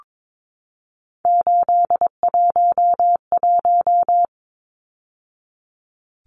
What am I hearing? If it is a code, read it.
Morse code, heard as a keyed tone, about 22 wpm, 700 Hz. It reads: 811
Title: Numbers - 3-Digit 22wpm